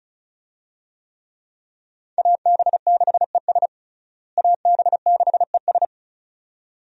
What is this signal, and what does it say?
Morse code, heard as a keyed tone, about 36 wpm, 700 Hz. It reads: AB6ES AB6ES
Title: USA Call Signs - 5 Characters - Repeated 2x - 35wpm